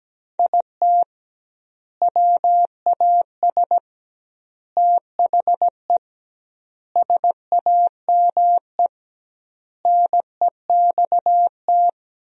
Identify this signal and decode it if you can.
Morse code, heard as a keyed tone, about 17 wpm, 700 Hz. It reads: IT WAS THE SAME NEXT
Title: 2x Word-Spacing - Sentences from Top 500 Words - 17wpm